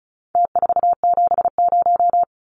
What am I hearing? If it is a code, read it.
Morse code, heard as a keyed tone, about 34 wpm, 700 Hz. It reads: T470
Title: Numbers - Contest Style - 4-Digit 35wpm